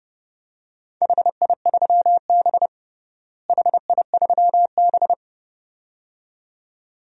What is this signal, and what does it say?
Morse code, heard as a keyed tone, about 30 wpm, 700 Hz. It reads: HI3B HI3B
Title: DXCC-100 Call Signs - 4 Characters - Repeated 2x - 30wpm